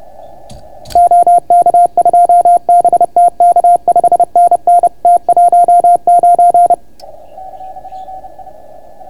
Morse code, heard as a keyed tone, about 30 wpm, 690 Hz. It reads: OK2BTK 5NN T19